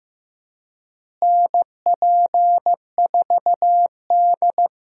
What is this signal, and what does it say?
Morse code, heard as a keyed tone, about 15 wpm, 700 Hz. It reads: NP4D
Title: USA Call Signs - 4 Characters - Repeated 3x - 15wpm